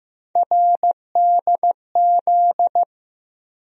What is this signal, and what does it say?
Morse code, heard as a keyed tone, about 15 wpm, 700 Hz. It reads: RDZ
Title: USA Call Sign Postfixes - 3 Letters 15wpm